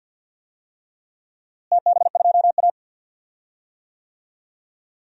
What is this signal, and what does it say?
Morse code, heard as a keyed tone, about 51 wpm, 700 Hz. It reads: TB2A
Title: DXCC-100 Call Signs - 4 Characters - Repeated 3x - 50wpm